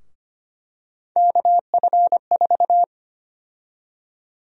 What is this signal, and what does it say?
Morse code, heard as a keyed tone, about 25 wpm, 700 Hz. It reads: KF4
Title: USA Call Sign Prefixes - Repeated 2x - 25wpm